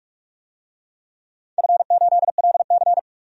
Morse code, heard as a keyed tone, about 42 wpm, 700 Hz. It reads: F8LC
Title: DXCC-100 Call Signs - 4 Characters - 45wpm